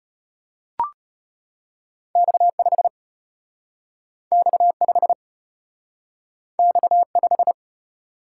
Morse code, heard as a keyed tone, about 33 wpm, 700 Hz. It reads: X5 X5 X5